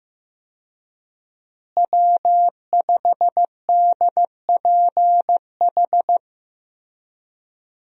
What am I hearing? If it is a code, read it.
Morse code, heard as a keyed tone, about 15 wpm, 700 Hz. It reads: W5DPH
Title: USA Call Signs - 5 Characters - Repeated 2x - 15wpm